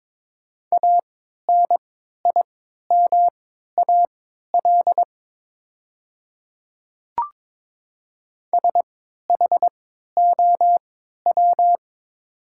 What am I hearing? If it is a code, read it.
Morse code, heard as a keyed tone, about 22 wpm, spaced as for 12 wpm, 700 Hz. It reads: ANIMAL SHOW
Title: 3x Character Spacing - Top 200 Words - 22wpm